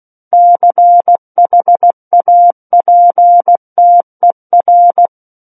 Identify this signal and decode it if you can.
Morse code, heard as a keyed tone, about 16 wpm, 700 Hz. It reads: CHAPTER
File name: Book One / 07-Chapter 7 - 16 WPM